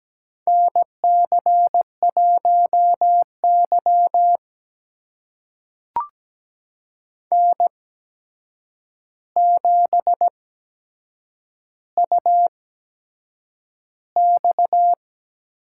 Morse code, heard as a keyed tone, about 17 wpm, spaced as for 7 wpm, 700 Hz. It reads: NC1Y N7UX
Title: USA Call Signs - 4 Characters - 8x Character Spacing - 17wpm